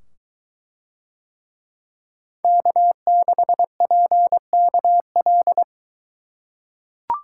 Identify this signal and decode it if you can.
Morse code, heard as a keyed tone, about 23 wpm, 700 Hz. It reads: K6PKL